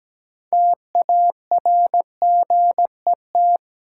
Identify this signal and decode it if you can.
Morse code, heard as a keyed tone, about 17 wpm, 700 Hz. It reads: TARGET
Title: Sentences from Top 1200 Words - New Word Drill - 17wpm